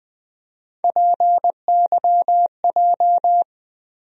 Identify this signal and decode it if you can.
Morse code, heard as a keyed tone, about 20 wpm, 700 Hz. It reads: PYJ
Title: USA Call Sign Postfixes - 3 Letters 20wpm